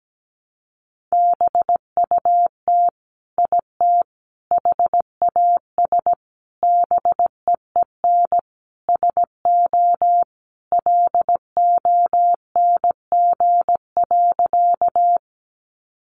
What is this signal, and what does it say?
Morse code, heard as a keyed tone, about 17 wpm, 700 Hz. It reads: BUT IT HAS BEEN SO LONG.
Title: Sentences from Top 100 Words 17wpm